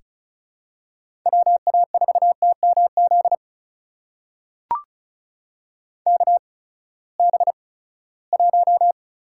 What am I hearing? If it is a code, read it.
Morse code, heard as a keyed tone, about 34 wpm, spaced as for 20 wpm, 700 Hz. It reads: WA4TMZ KB1
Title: USA Call Signs - 6 Characters - 8x Character Spacing - 35wpm